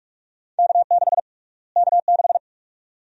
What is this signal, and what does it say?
Morse code, heard as a keyed tone, about 45 wpm, 700 Hz. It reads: K6 K6
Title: USA Call Sign Prefixes - Repeated 2x - 45wpm